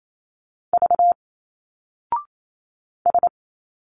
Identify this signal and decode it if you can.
Morse code, heard as a keyed tone, about 28 wpm, 700 Hz. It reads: V S
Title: Single Letters 28wpm